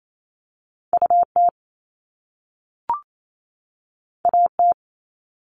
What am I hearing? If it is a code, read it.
Morse code, heard as a keyed tone, about 28 wpm, 700 Hz. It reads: UT AT
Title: Top Two Letter Combinations in Words 28wpm